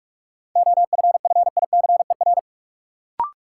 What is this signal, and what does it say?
Morse code, heard as a keyed tone, about 43 wpm, 700 Hz. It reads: OFFICER
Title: Sentences from Top 1200 Words - New Word Drill Alt - 45wpm